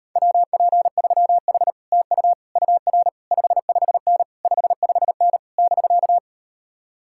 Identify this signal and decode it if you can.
Morse code, heard as a keyed tone, about 38 wpm, 700 Hz. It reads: WP3H TU UR 55N 55N <BK>